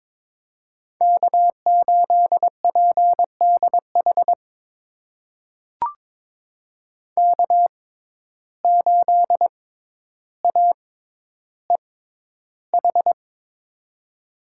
Morse code, heard as a keyed tone, about 22 wpm, spaced as for 11 wpm, 700 Hz. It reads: K8PDH K8AEH
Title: USA Call Signs - 5 Characters - 6x Character Spacing - 22wpm